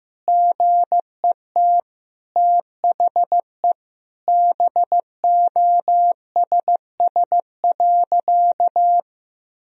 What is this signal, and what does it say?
Morse code, heard as a keyed tone, about 15 wpm, 700 Hz. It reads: GET THE BOSS.
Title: Sentences from Top 1000 Words - Review Part 4 - 15wpm